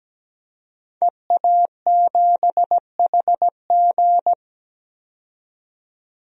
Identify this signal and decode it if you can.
Morse code, heard as a keyed tone, about 17 wpm, 700 Hz. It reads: EA7HG
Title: DXCC-100 Call Signs - 5 Characters - 17wpm